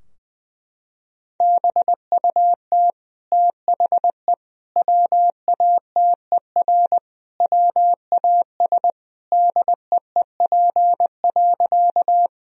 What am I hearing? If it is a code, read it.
Morse code, heard as a keyed tone, about 21 wpm, 700 Hz. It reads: BUT THE WATER WAS DEEP.
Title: Sentences from Top 1000 Words - Review Part 3 - 20wpm